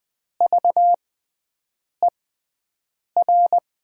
Morse code, heard as a keyed tone, about 20 wpm, spaced as for 7 wpm, 700 Hz. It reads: VER
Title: 6x Character Spacing - Top 500 Words - 20wpm